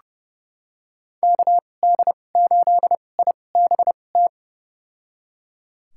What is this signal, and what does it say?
Morse code, heard as a keyed tone, about 30 wpm, spaced as for 22 wpm, 700 Hz. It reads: KD8IBT
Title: USA Call Signs - 6 Characters - 2x Character Spacing - 30wpm